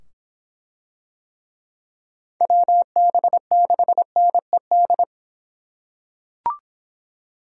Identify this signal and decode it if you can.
Morse code, heard as a keyed tone, about 25 wpm, 700 Hz. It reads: WB6NED